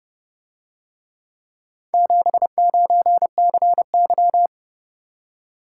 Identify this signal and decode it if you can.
Morse code, heard as a keyed tone, about 30 wpm, 700 Hz. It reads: 79CY